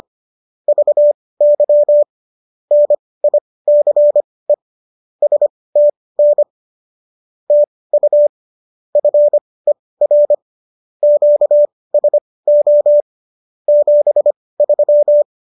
Morse code, heard as a keyed tone, about 25 wpm, spaced as for 18 wpm, 590 Hz. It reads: VY NICE STN TU FER QSO 73